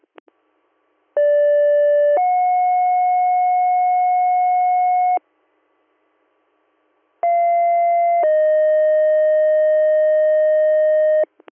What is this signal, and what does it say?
Two-tone page, heard as a one-second tone, then a three-second tone, about 580 then 730 Hz, about 690 then 620 Hz. It is heard from 1.2 to 11.2 s.